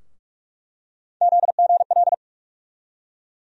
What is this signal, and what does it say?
Morse code, heard as a keyed tone, about 47 wpm, 700 Hz. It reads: ZGL
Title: USA Call Sign Postfixes - 3 Letters 45wpm